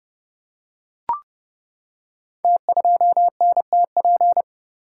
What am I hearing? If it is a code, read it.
Morse code, heard as a keyed tone, about 30 wpm, 700 Hz. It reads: T2NTP
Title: Code Groups - 5 Letter:Numbers - 30wpm